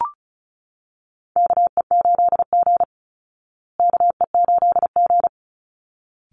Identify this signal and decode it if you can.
Morse code, heard as a keyed tone, about 34 wpm, 700 Hz. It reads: KE8G KE8G
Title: Call Signs Repeated 3x - 35wpm